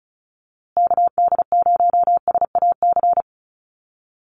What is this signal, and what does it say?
Morse code, heard as a keyed tone, about 34 wpm, 700 Hz. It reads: KD0SAC
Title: USA Call Signs Repeated 2x 35wpm